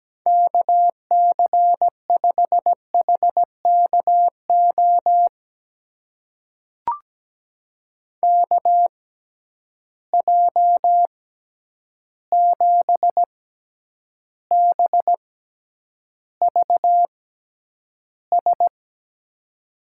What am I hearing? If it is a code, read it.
Morse code, heard as a keyed tone, about 17 wpm, spaced as for 9 wpm, 700 Hz. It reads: KC5HKO KJ7BVS
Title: USA Call Signs - 6 Characters - 6x Character Spacing - 17wpm